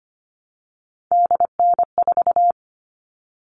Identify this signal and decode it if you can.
Morse code, heard as a keyed tone, about 25 wpm, 700 Hz. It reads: DN4
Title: All Cut Numbers - 3-Digit 25wpm